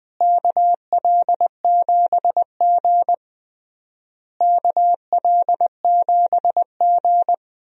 Morse code, heard as a keyed tone, about 21 wpm, 700 Hz. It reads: KL7G KL7G
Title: DXCC-100 Call Signs - 4 Characters - Repeated 3x - 20wpm